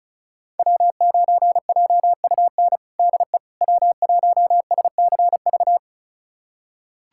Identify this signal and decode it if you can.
Morse code, heard as a keyed tone, about 36 wpm, 700 Hz. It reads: W9JUN DE W1SCV
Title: USA Callsign DE Callsign - 5 Characters - 35wpm